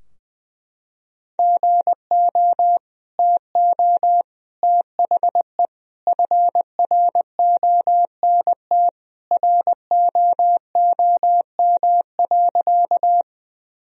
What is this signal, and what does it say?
Morse code, heard as a keyed tone, about 20 wpm, 700 Hz. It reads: GO TO THE FRONT ROOM.
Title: Sentences from Top 1000 Words - Review Part 2 - 20wpm